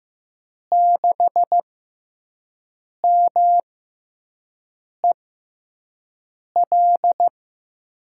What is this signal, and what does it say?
Morse code, heard as a keyed tone, about 15 wpm, spaced as for 5 wpm, 700 Hz. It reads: 6MEL
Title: USA Call Signs - 6 Characters - 6x Character Spacing - 15wpm